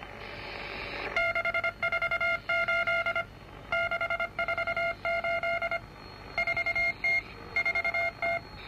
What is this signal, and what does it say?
Morse code, heard as a keyed tone, about 25 wpm, 2150 Hz, with noise. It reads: 648 648 4T 4T